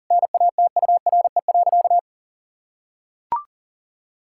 Morse code, heard as a keyed tone, about 40 wpm, 700 Hz. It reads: NATURE.
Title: Sentences from Top 1000 Words - Review Part 5 - 40wpm